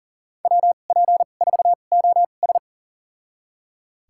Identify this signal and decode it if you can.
Morse code, heard as a keyed tone, about 40 wpm, spaced as for 29 wpm, 700 Hz. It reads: WP4OS
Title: USA Call Signs - 5 Characters - 2x Character Spacing - 40wpm